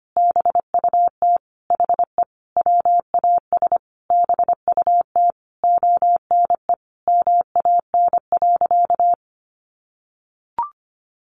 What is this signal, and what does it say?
Morse code, heard as a keyed tone, about 25 wpm, 700 Hz. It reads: BUT HE WAS BUT ONE MAN.
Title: Sentences from Top 1500 Words 25wpm